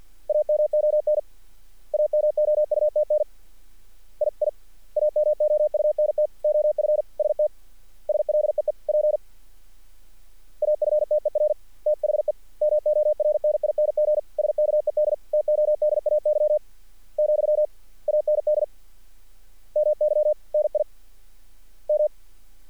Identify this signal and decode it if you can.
Morse code, heard as a keyed tone, about 50 wpm, 600 Hz. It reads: AMON AMOUTN I I AMOUNT OF ST SLEEP AFTER THE MORNING SKED TODAY , AND MY NI M
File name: archived / CRSnet